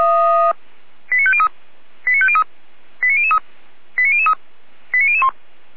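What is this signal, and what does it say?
Five-tone selective call (ZVEI), heard as quick runs of five tones, tones of about 70 ms. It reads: 89503 89503 89003 89003 89001
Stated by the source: ZVEI five-tone selective call received by SDR, fire brigade Orlamünde (feuerwehr-orlamuende) server.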